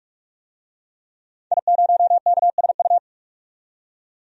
Morse code, heard as a keyed tone, about 46 wpm, 700 Hz. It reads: I0KSU
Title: DXCC-100 Call Signs - 5 Characters - 45wpm